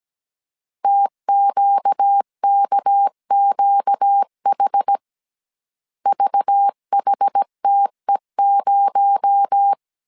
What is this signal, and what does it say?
Morse code, heard as a keyed tone, about 17 wpm, 800 Hz. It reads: TQKQH VHTE0
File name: CW / DET / 1 6 1500C S A 16wpm 2635